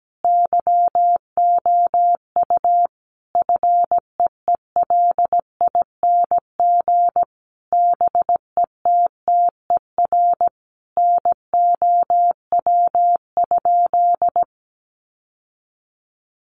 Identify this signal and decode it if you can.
Morse code, heard as a keyed tone, about 17 wpm, 700 Hz. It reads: YOU FEELING BETTER NOW?
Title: Sentences from Top 1200 Words 17wpm